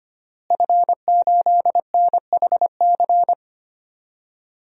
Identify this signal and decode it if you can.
Morse code, heard as a keyed tone, about 25 wpm, 700 Hz. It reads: F8NHC
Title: DXCC-100 Call Signs - 5 Characters - Repeated 3x - 25wpm